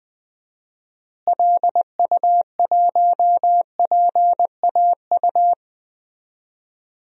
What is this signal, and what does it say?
Morse code, heard as a keyed tone, about 20 wpm, 700 Hz. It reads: LU1PAU
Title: DXCC-100 Call Signs - 6 Characters - Repeated 2x - 20wpm